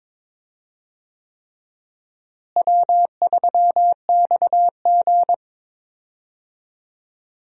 Morse code, heard as a keyed tone, about 22 wpm, 700 Hz. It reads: W3XG
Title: USA Call Signs - 4 Characters - Repeated 3x - 22wpm